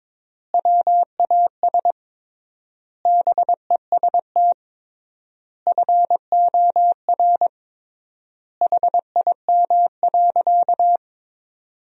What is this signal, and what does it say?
Morse code, heard as a keyed tone, about 22 wpm, 700 Hz. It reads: WAS BEST FOR HIM.
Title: 3x Word-Spacing - Sentences from Top 500 Words - 22wpm